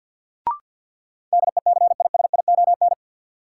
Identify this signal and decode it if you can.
Morse code, heard as a keyed tone, about 52 wpm, 700 Hz. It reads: DECISION
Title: Sentences from Top 1000 Words - New Word Drill - 50wpm